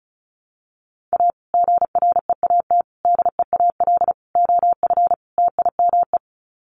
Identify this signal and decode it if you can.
Morse code, heard as a keyed tone, about 34 wpm, 700 Hz. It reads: A GREAT DEAL OF TIME
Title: Sets of 5 Words Encore2 35wpm